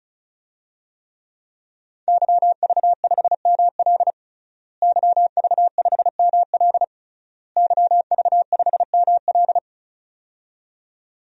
Morse code, heard as a keyed tone, about 36 wpm, 700 Hz. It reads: YV5ML YV5ML YV5ML
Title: DXCC-100 Call Signs - 5 Characters - Repeated 3x - 35wpm